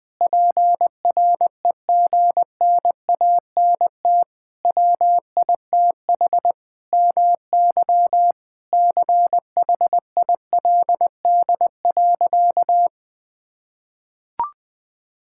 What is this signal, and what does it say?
Morse code, heard as a keyed tone, about 21 wpm, 700 Hz. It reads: PREGNANT WITH MY CHILD.
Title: Sentences from Top 1000 Words - Review Part 6 - 20wpm